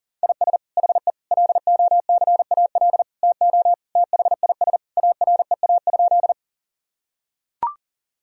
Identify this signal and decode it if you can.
Morse code, heard as a keyed tone, about 40 wpm, 700 Hz. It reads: IS HE LOCAL TO THIS AREA?
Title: Sentences from Top 1000 Words - Review Part 3 - 40wpm